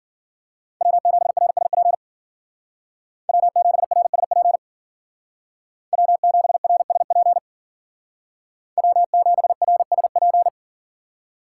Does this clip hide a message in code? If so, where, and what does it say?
Yes: Morse code from 0.8 to 10.5 s, heard as a keyed tone, about 52 wpm, 700 Hz. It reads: W7RSP W7RSP W7RSP W7RSP